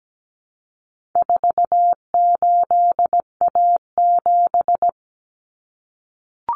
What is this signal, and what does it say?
Morse code, heard as a keyed tone, about 17 wpm, 700 Hz. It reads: 48A7
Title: Common Cut Numbers - 4-Digit 17wpm